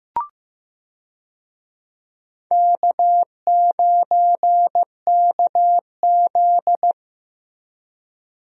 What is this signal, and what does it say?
Morse code, heard as a keyed tone, about 15 wpm, 700 Hz. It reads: K9KZ
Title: USA Call Signs - 4 Characters - Repeated 3x - 15wpm